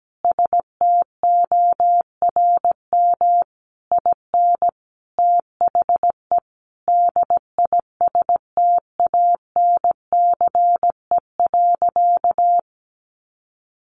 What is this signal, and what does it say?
Morse code, heard as a keyed tone, about 17 wpm, 700 Hz. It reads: STORM IN THE DISTANCE.
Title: Sentences from Top 1500 Words 17wpm